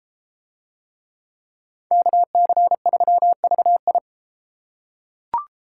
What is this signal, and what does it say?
Morse code, heard as a keyed tone, about 33 wpm, 700 Hz. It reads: KC3VI